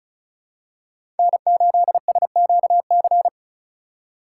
Morse code, heard as a keyed tone, about 36 wpm, 700 Hz. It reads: N8SQC
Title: USA Call Signs - 5 Characters - 2x Character Spacing - 35wpm